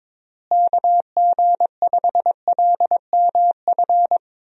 Morse code, heard as a keyed tone, about 22 wpm, 700 Hz. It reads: KG5LMF
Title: USA Call Signs - 6 Characters - Repeated 2x - 22wpm